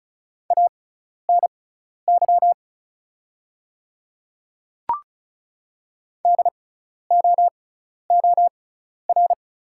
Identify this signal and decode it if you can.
Morse code, heard as a keyed tone, about 34 wpm, spaced as for 12 wpm, 690 Hz. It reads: ANY DOOR
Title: 6x Character Spacing - Top 300 Words - 35wpm